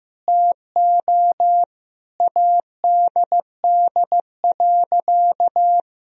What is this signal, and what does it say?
Morse code, heard as a keyed tone, about 14 wpm, 700 Hz. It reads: TO ADD.
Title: Sentences from Top 1000 Words - Review Part 1 - 15wpm